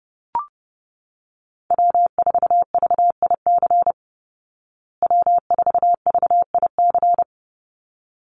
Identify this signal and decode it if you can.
Morse code, heard as a keyed tone, about 30 wpm, 700 Hz. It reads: W4VIC W4VIC
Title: Call Signs Repeated 3x - 30wpm